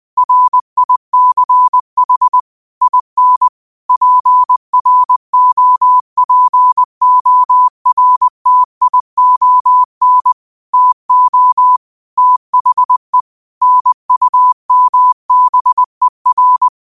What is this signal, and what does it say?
Morse code, heard as a keyed tone, about 20 wpm, 1000 Hz. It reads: RICH IN PROPORTION TO THE NUMBER